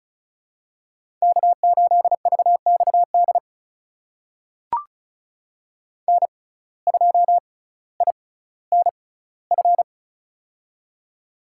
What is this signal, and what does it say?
Morse code, heard as a keyed tone, about 36 wpm, spaced as for 18 wpm, 700 Hz. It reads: K8VXD N2INF
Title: USA Call Signs - 5 Characters - 6x Character Spacing - 35wpm